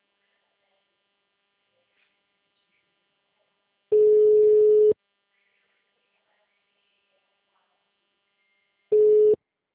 Ringback tone (Europe, 425 Hz).